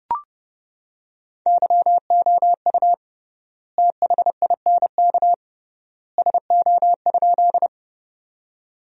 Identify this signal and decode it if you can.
Morse code, heard as a keyed tone, about 30 wpm, 700 Hz. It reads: YOU THINK SO?
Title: 3x Word-Spacing - Sentences from Top 200 Words - 30wpm